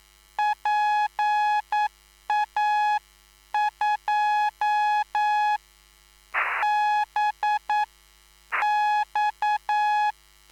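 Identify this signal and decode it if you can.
Morse code, heard as a keyed tone, about 9 wpm, spaced as for 7 wpm, 840 Hz. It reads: PA2BX
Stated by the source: Live CW QSO between PA2BX and SP3JFK on the 14055 kHz QRS frequency, at a basic 8 wpm.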